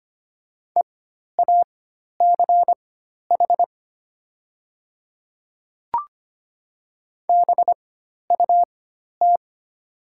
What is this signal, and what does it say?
Morse code, heard as a keyed tone, about 25 wpm, spaced as for 12 wpm, 700 Hz. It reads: EACH BUT
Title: 4x Character Spacing - Top 200 Words - 25wpm